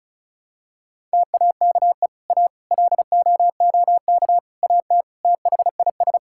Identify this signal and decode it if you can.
Morse code, heard as a keyed tone, about 34 wpm, 700 Hz. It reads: TAKE A LOOK AT THIS
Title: Sentences from Top 1000 Words - Review Part 5 - 35wpm